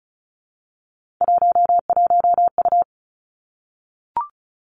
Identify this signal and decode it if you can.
Morse code, heard as a keyed tone, about 36 wpm, 700 Hz. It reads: 11U
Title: Three Challenging Letter-Number Combinations 35wpm